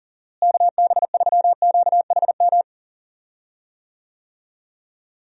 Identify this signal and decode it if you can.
Morse code, heard as a keyed tone, about 40 wpm, 700 Hz. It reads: KB3QHM
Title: USA Call Signs - 6 Characters - Repeated 3x - 40wpm